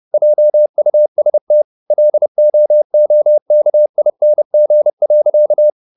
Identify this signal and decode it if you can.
Morse code, heard as a keyed tone, about 30 wpm, 600 Hz. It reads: JUST LOOKING.
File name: the old man and the sea / the old man and the sea 1 30wpm